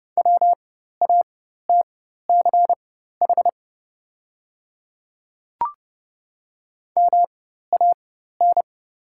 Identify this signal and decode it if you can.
Morse code, heard as a keyed tone, about 30 wpm, spaced as for 14 wpm, 700 Hz. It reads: WATCH MAN
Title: 4x Character Spacing - Top 300 Words - 30wpm